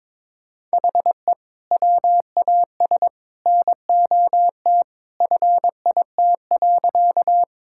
Morse code, heard as a keyed tone, about 22 wpm, 700 Hz. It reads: HE WAS NOT FIT.
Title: Sentences from Top 1000 Words - Review Part 5 - 22wpm